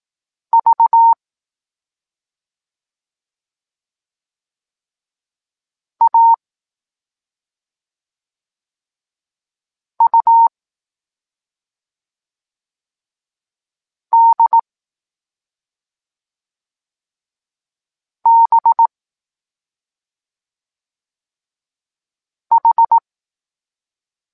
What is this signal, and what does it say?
Morse code, heard as a keyed tone, about 18 wpm, 920 Hz. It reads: V A U D B H
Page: Tekst 14 – doskonalenie odbioru poznanych liter